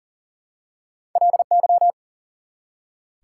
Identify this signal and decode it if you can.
Morse code, heard as a keyed tone, about 40 wpm, 700 Hz. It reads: LY
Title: Prefixes and Suffixes 40wpm